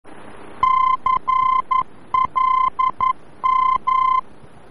Morse code, heard as a keyed tone, about 11 wpm, 1040 Hz. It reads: CLM
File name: france / mx / CLM-VOR-F